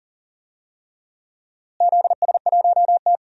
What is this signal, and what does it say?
Morse code, heard as a keyed tone, about 40 wpm, 700 Hz. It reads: ZS1T
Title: DXCC-100 Call Signs - 4 Characters - Repeated 3x - 40wpm